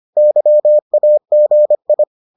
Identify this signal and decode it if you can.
Morse code, heard as a keyed tone, about 25 wpm, 600 Hz. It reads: YAGI